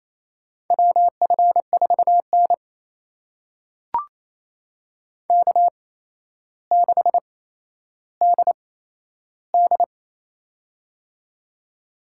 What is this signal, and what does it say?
Morse code, heard as a keyed tone, about 28 wpm, spaced as for 12 wpm, 700 Hz. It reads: WF4N K6DD
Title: USA Call Signs - 4 Characters - 8x Character Spacing - 28wpm